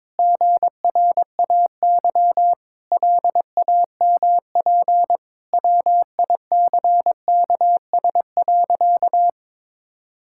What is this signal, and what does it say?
Morse code, heard as a keyed tone, about 22 wpm, 700 Hz. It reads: GRAY LAMP WICKS.
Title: Pangram Challenge v2 - 22wpm